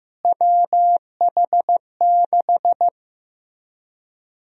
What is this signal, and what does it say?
Morse code, heard as a keyed tone, about 15 wpm, 700 Hz. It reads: WH6
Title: USA Call Sign Prefixes - Repeated 2x - 15wpm